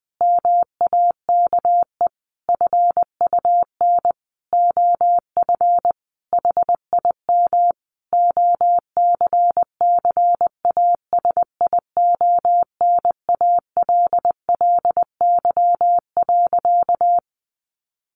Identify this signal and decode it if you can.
Morse code, heard as a keyed tone, about 20 wpm, 700 Hz. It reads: MAKE FUN OF HIM OCCASIONALLY.
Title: Sentences from Top 1300 Words 20wpm